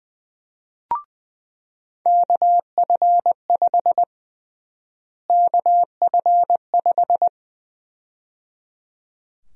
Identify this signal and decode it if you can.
Morse code, heard as a keyed tone, about 21 wpm, 700 Hz. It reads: KF5 KF5
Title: USA Call Sign Prefixes - Repeated 2x - 20wpm